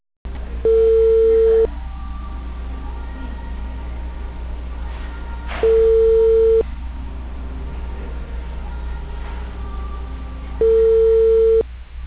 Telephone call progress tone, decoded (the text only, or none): ringback tone